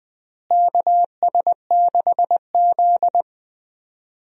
Morse code, heard as a keyed tone, about 21 wpm, 700 Hz. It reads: KS6Z
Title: USA Call Signs - 4 Characters - Repeated 2x - 20wpm